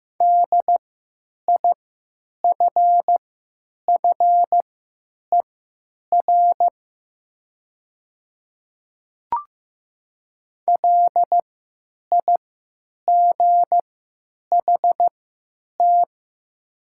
Morse code, heard as a keyed tone, about 15 wpm, spaced as for 9 wpm, 700 Hz. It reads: DIFFER LIGHT
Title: 3x Character Spacing - Top 200 Words - 15wpm